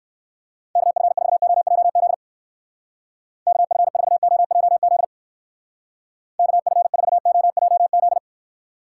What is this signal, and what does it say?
Morse code, heard as a keyed tone, about 58 wpm, 700 Hz. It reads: KF4OJZ KF4OJZ KF4OJZ